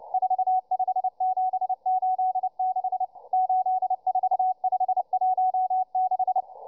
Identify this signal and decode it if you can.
Morse code, heard as a keyed tone, about 29 wpm, 750 Hz. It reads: 45786 84516